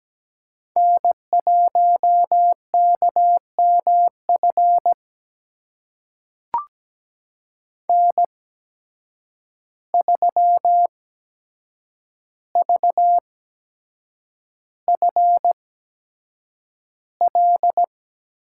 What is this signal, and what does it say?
Morse code, heard as a keyed tone, about 17 wpm, spaced as for 7 wpm, 700 Hz. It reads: N1KMF N3VFL
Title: USA Call Signs - 5 Characters - 8x Character Spacing - 17wpm